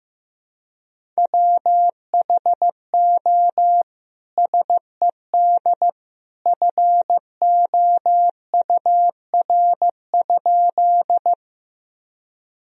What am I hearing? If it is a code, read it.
Morse code, heard as a keyed tone, about 15 wpm, 700 Hz. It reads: WHO SED FOUR?
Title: Sentences from Top 300 Words and Abbreviations 15wpm